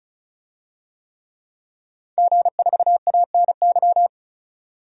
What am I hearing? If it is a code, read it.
Morse code, heard as a keyed tone, about 34 wpm, 700 Hz. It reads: G4ANY
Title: DXCC-100 Call Signs - 5 Characters - 35wpm